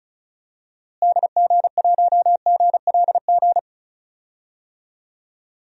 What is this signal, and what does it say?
Morse code, heard as a keyed tone, about 36 wpm, 700 Hz. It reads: DG1GLG
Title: DXCC-100 Call Signs - 6 Characters - Repeated 2x - 35wpm